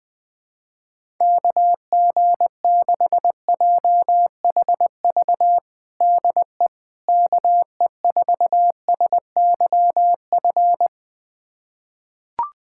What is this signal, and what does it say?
Morse code, heard as a keyed tone, about 21 wpm, 700 Hz. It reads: KG6JHV DE KE4SYF
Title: USA Callsign DE Callsign - 6 Characters - 20wpm